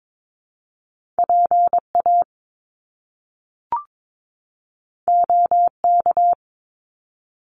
Morse code, heard as a keyed tone, about 22 wpm, 700 Hz. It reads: PA OK
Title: Top Two Letter Combinations in Words 22wpm